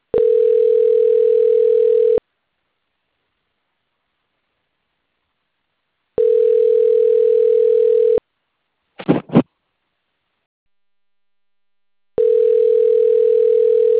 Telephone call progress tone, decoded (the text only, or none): ringback tone